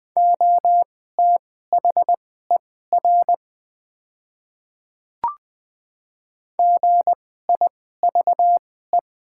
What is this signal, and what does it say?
Morse code, heard as a keyed tone, about 20 wpm, spaced as for 15 wpm, 700 Hz. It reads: OTHER GIVE
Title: 2x Character Spacing - Top 200 Words - 20wpm